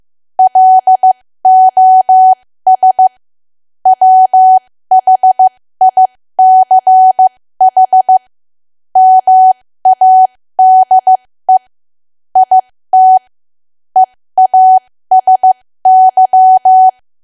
Morse code, heard as a keyed tone, about 15 wpm, spaced as for 13 wpm, 750 Hz. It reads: LOS WHICH MADE IT EASY
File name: cwa / W1AW-Level2 / 13WPM